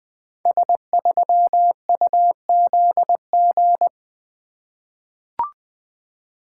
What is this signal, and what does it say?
Morse code, heard as a keyed tone, about 20 wpm, 700 Hz. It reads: S3UZG
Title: Code Groups - 5 Letter:Numbers - 20wpm